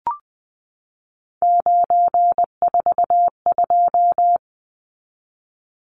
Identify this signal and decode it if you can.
Morse code, heard as a keyed tone, about 20 wpm, 700 Hz. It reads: 942
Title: Numbers - 3-Digit 20wpm